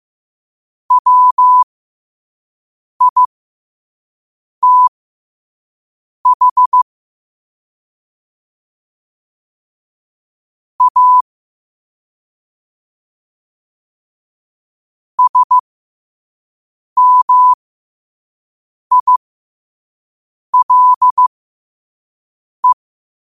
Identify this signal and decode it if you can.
Morse code, heard as a keyed tone, about 15 wpm, spaced as for 5 wpm, 1000 Hz. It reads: WITH A SMILE
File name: qotdfiles / 5wpm